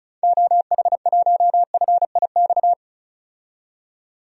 Morse code, heard as a keyed tone, about 36 wpm, 700 Hz. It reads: OH1FIX
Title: DXCC-100 Call Signs - 6 Characters - 35wpm